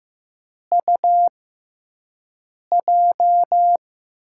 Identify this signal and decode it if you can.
Morse code, heard as a keyed tone, about 15 wpm, spaced as for 5 wpm, 700 Hz. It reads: UJ
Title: USA Call Signs - 6 Characters - 6x Character Spacing - 15wpm